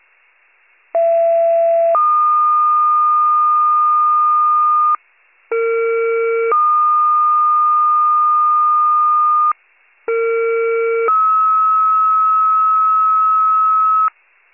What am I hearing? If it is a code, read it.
Two-tone page, heard as a one-second tone, then a three-second tone, about 670 then 1220 Hz, about 460 then 1220 Hz, about 460 then 1290 Hz. It is heard from 0.9 to 14.1 s.